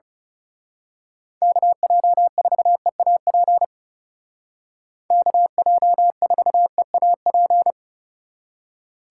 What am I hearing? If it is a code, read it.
Morse code, heard as a keyed tone, about 34 wpm, 700 Hz. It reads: KJ4EAP KJ4EAP
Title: Speed Racing - USA 2x3 Call Signs 30wpm